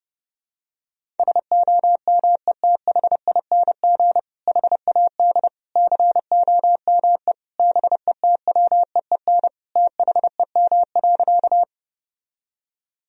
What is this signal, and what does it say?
Morse code, heard as a keyed tone, about 30 wpm, 700 Hz. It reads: SOMETHING HAD COME BETWEEN THEM.
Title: Sentences from Top 1000 Words - Review Part 1 - 30wpm